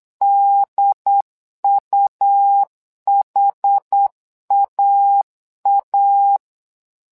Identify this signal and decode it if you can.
Morse code, heard as a keyed tone, about 8 wpm, 800 Hz. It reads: DUHAA
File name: LEZIONI / 2 UHDN / 2 UHD 8wpm 2258